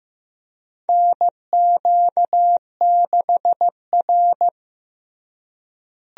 Morse code, heard as a keyed tone, about 15 wpm, 700 Hz. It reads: NQ6R
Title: USA Call Signs - 4 Characters - Repeated 2x - 15wpm